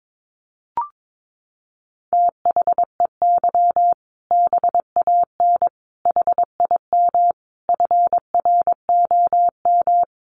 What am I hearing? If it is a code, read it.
Morse code, heard as a keyed tone, about 22 wpm, 700 Hz. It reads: THEY BAN HIM FROM
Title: Easy Phrases 22wpm